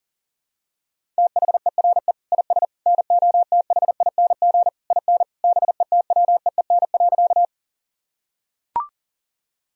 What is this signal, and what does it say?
Morse code, heard as a keyed tone, about 40 wpm, 700 Hz. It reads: THERE IS NOTHING IN BETWEEN.
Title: Sentences from Top 1000 Words - Review Part 1 - 40wpm